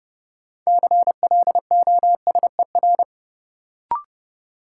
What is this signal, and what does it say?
Morse code, heard as a keyed tone, about 30 wpm, 700 Hz. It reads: CLOSER
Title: Sets of 5 Words Encore - New Word Drill - 30wpm